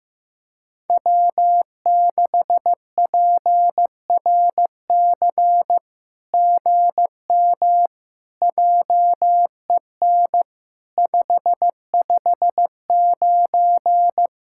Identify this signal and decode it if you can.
Morse code, heard as a keyed tone, about 15 wpm, 700 Hz. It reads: W6PRC GM JEN 559
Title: SOTA QSO Practice 15wpm